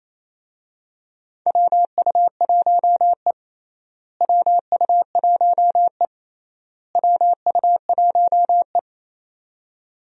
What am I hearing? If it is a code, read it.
Morse code, heard as a keyed tone, about 28 wpm, 700 Hz. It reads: WU1E WU1E WU1E
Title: USA Call Signs - 4 Characters - Repeated 3x - 28wpm